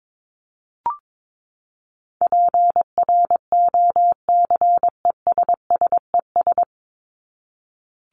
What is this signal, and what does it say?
Morse code, heard as a keyed tone, about 22 wpm, 700 Hz. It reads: PROCESSES
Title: ES:S Suffix - Top 100 Words with Base Words 22wpm